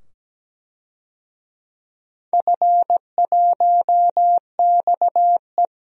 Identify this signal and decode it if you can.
Morse code, heard as a keyed tone, about 17 wpm, 700 Hz. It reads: F1XE
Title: DXCC-100 Call Signs - 4 Characters - 17wpm